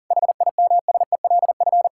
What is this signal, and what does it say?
Morse code, heard as a keyed tone, about 40 wpm, 700 Hz. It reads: HIMSELF
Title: Sentences from Top 1600 Words 40wpm